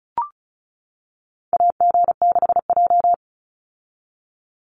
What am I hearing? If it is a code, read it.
Morse code, heard as a keyed tone, about 34 wpm, 700 Hz. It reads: AG6J
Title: USA Call Signs 35wpm